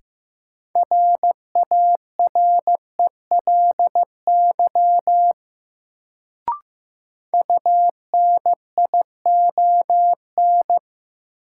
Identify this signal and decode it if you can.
Morse code, heard as a keyed tone, about 15 wpm, 700 Hz. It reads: RARELY UNION
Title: Sentences from Top 1400 Words - New Word Drill - 15wpm